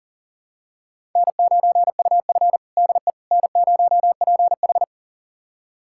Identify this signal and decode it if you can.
Morse code, heard as a keyed tone, about 40 wpm, 700 Hz. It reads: N9UF DE N0PH
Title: USA Callsign DE Callsign - 4 Characters - 40wpm